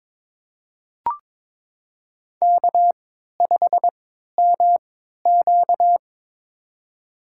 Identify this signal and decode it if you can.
Morse code, heard as a keyed tone, about 22 wpm, spaced as for 12 wpm, 700 Hz. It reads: K5MQ
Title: USA Call Signs - 4 Characters - 3x Character Spacing - 22wpm